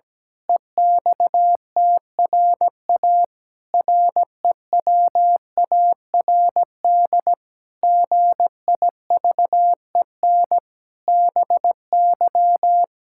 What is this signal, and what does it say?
Morse code, heard as a keyed tone, about 17 wpm, 700 Hz. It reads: EXTRA REWARD GIVEN BY
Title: Pangram Challenge v2 - 17wpm